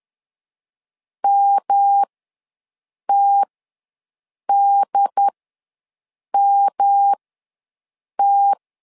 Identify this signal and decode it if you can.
Morse code, heard as a keyed tone, about 10 wpm, spaced as for 6 wpm, 800 Hz. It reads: MTDMT
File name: LEZIONI / 3 MWRO / 3 M ws 10wpm 2121